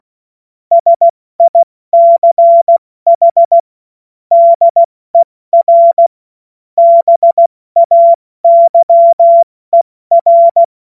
Morse code, heard as a keyed tone, about 16 wpm, spaced as for 14 wpm, 680 Hz. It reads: SICH DER BAYER
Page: cw-witze